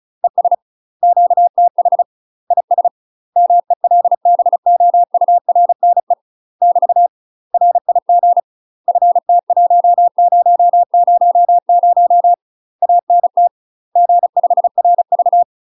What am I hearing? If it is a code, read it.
Morse code, heard as a keyed tone, about 36 wpm, 700 Hz. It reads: ES QTH IS MELBOURNE <BT> RIG FT1000 ANT G5RV
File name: UK MASTER QSOs TRAINING / 35WPM / UK MASTER QSOs 0040